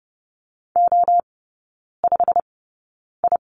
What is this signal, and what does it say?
Morse code, heard as a keyed tone, about 30 wpm, spaced as for 9 wpm, 700 Hz. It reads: O5I
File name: Lesson 15e - Number 5 - Three Characters 30wpm